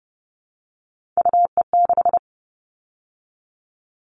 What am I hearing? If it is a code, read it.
Morse code, heard as a keyed tone, about 30 wpm, 700 Hz. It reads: UE6
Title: All Cut Numbers - 3-Digit 30wpm